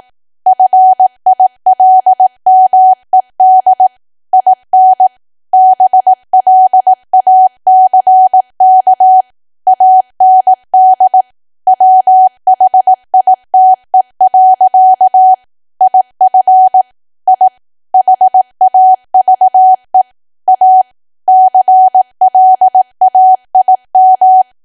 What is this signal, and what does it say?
Morse code, heard as a keyed tone, about 18 wpm, 750 Hz. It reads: FILMED IN BLACK AND WHITE. IF I HAVE A CLAIM
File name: Archive / 18 WPM / 18WPM